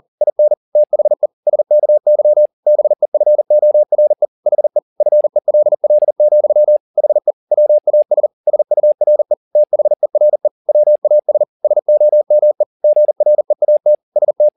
Morse code, heard as a keyed tone, about 40 wpm, 600 Hz. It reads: IN THE SKY BEFORE HE FELL, HE WAS SURE THERE WAS SOME GREAT ST
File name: the old man and the sea 2 40wpm